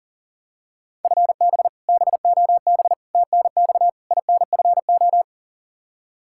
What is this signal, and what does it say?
Morse code, heard as a keyed tone, about 40 wpm, 700 Hz. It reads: FB BOB TNX INFO
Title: Practice QSO - Exchange 3 - More Info - 40wpm